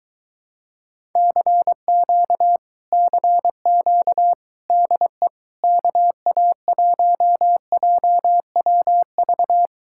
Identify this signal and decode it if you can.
Morse code, heard as a keyed tone, about 23 wpm, 700 Hz. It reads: CQ CQ DE KA1JWV